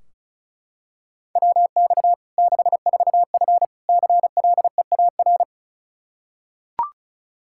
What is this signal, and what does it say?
Morse code, heard as a keyed tone, about 36 wpm, 700 Hz. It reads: WX 64F CLEAR
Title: Practice QSO - Exchange 2 - Info - 35wpm